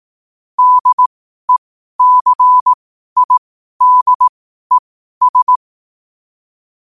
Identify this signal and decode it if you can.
Morse code, heard as a keyed tone, about 18 wpm, spaced as for 12 wpm, 1000 Hz. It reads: DECIDES
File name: qotdfiles / 12wpm